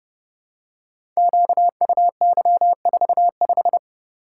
Morse code, heard as a keyed tone, about 29 wpm, 700 Hz. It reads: QUY45
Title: Code Groups - 5 Letter:Numbers - 30wpm